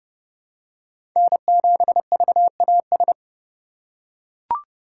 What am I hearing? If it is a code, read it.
Morse code, heard as a keyed tone, about 30 wpm, 700 Hz. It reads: N7VAS
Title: USA Call Signs - 5 Characters - 2x Character Spacing - 30wpm